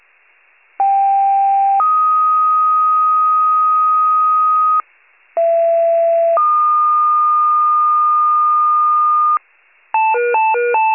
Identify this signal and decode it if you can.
Two-tone page, heard as a one-second tone, then a three-second tone, about 790 then 1250 Hz, about 670 then 1220 Hz. It is heard from 0.8 to 9.4 s.
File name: wcalert